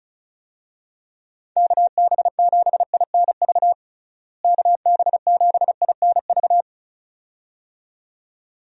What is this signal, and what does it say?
Morse code, heard as a keyed tone, about 36 wpm, 700 Hz. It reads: KB7INV KB7INV
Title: USA Call Signs - 6 Characters - Repeated 2x - 35wpm